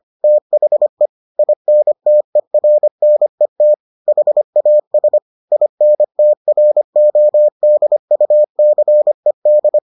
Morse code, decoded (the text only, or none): THE INTERNET HAS INTRODUCED